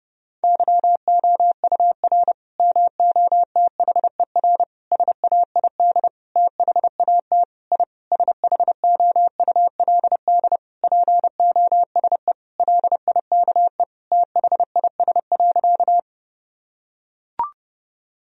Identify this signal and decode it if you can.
Morse code, heard as a keyed tone, about 31 wpm, 700 Hz. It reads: YOUR MOTHER SAID THAT I SHOULD POSE LIKE THIS.
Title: Sentences from Top 1000 Words - Review Part 2 - 30wpm